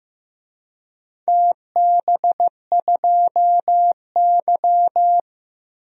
Morse code, heard as a keyed tone, about 15 wpm, 700 Hz. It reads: TB2Y
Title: DXCC-100 Call Signs - 4 Characters - Repeated 2x - 15wpm